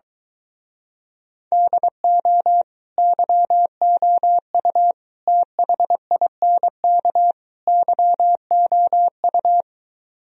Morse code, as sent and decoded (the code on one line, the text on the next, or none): -.. --- / -.-- --- ..- / - .... .. -. -.- / -.-- --- ..-
DO YOU THINK YOU